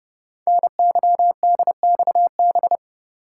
Morse code, heard as a keyed tone, about 30 wpm, 700 Hz. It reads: NYDXB
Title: Code Groups - 5 Letters - 30wpm